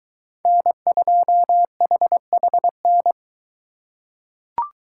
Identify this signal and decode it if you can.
Morse code, heard as a keyed tone, about 21 wpm, 700 Hz. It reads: N2HHN